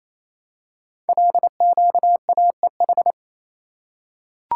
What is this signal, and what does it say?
Morse code, heard as a keyed tone, about 27 wpm, 700 Hz. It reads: LQAEH
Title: Code Groups - 5 Letters - 28wpm